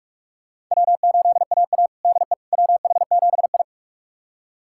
Morse code, heard as a keyed tone, about 46 wpm, 700 Hz. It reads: W8AA DE WH7I
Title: USA Callsign DE Callsign - 4 Characters - 45wpm